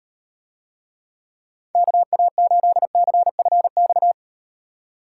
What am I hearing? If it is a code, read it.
Morse code, heard as a keyed tone, about 38 wpm, 700 Hz. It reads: KA8CFX